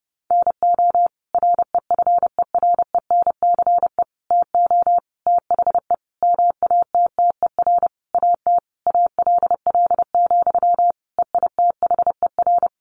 Morse code, heard as a keyed tone, about 30 wpm, 700 Hz. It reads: NO REFERENCE TO THE MATTER AT ALL, EITHER
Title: Sentences from Top 1200 Words 30wpm